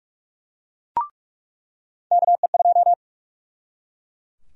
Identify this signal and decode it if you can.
Morse code, heard as a keyed tone, about 43 wpm, 700 Hz. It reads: KE2
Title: USA Call Sign Prefixes - 45wpm